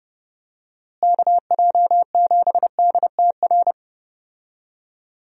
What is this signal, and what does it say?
Morse code, heard as a keyed tone, about 30 wpm, 700 Hz. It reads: KJ7DTR
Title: USA Call Signs - 6 Characters - Repeated 3x - 30wpm